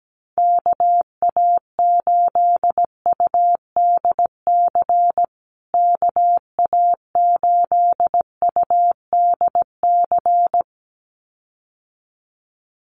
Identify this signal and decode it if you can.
Morse code, heard as a keyed tone, about 17 wpm, 700 Hz. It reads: KA8UDC KA8UDC
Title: USA Call Signs Repeated 2x 17wpm